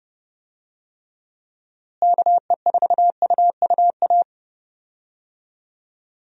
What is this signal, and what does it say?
Morse code, heard as a keyed tone, about 32 wpm, 700 Hz. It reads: KE4UUA
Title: USA Call Signs - 6 Characters - Repeated 2x - 30wpm